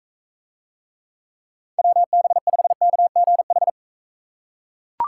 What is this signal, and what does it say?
Morse code, heard as a keyed tone, about 40 wpm, 700 Hz. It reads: WB5KGH